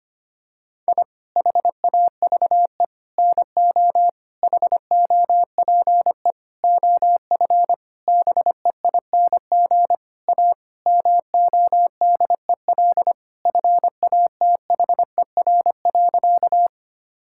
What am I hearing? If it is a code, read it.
Morse code, heard as a keyed tone, about 25 wpm, 700 Hz. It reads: I HAVE NO HOPE OF BEING A MODEL FATHER.
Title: Sentences from Top 1000 Words - Review Part 5 - 25wpm